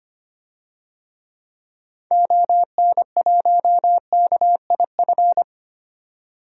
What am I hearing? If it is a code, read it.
Morse code, heard as a keyed tone, about 25 wpm, 700 Hz. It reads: ON1KIF
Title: DXCC-100 Call Signs - 6 Characters - 25wpm